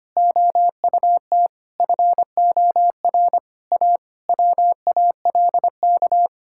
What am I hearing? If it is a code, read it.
Morse code, heard as a keyed tone, about 25 wpm, 700 Hz. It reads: OUT FOR A WALK
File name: practice / 25wpm / Difficult Sentences with Hints - 25wpm